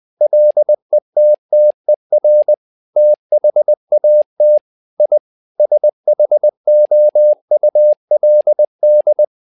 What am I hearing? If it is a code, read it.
Morse code, heard as a keyed tone, about 20 wpm, 600 Hz. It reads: LETTER THAT I SHOULD
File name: learning center / tower trouble / tower trouble 20wpm